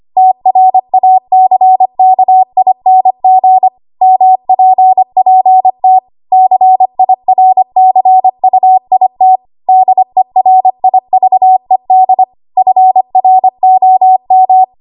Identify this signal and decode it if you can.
Morse code, heard as a keyed tone, about 25 wpm, 750 Hz. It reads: TRACKING MPPT CIRCUIT DERIVED FROM
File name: Archive / 25 WPM / 25WPM